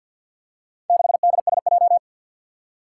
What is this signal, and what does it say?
Morse code, heard as a keyed tone, about 52 wpm, 700 Hz. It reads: 6DSJ
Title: Four Challenging Letter-Number Combinations 50wpm